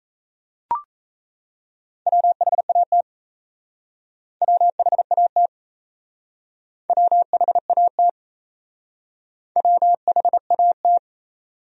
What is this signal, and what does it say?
Morse code, heard as a keyed tone, about 33 wpm, 700 Hz. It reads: WHAT WHAT WHAT WHAT
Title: Speed-Racing Top 100 Words 28wpm